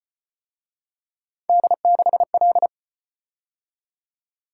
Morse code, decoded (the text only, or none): D6L